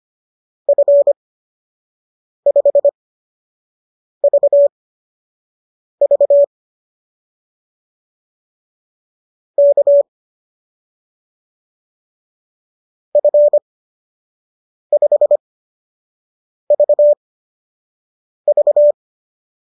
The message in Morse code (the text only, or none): F5VV K F5VV